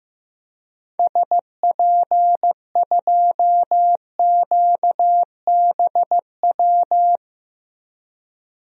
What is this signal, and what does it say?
Morse code, heard as a keyed tone, about 15 wpm, 700 Hz. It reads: SP2QBW
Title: DXCC-100 Call Signs - 6 Characters - 15wpm